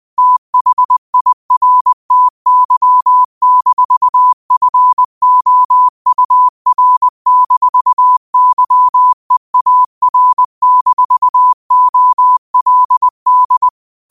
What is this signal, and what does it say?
Morse code, heard as a keyed tone, about 20 wpm, 1000 Hz. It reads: THIRTY-FOUR-YEAR-OLD